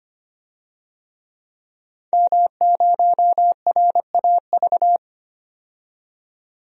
Morse code, heard as a keyed tone, about 25 wpm, 700 Hz. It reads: M0RAV
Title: DXCC-100 Call Signs - 5 Characters - 25wpm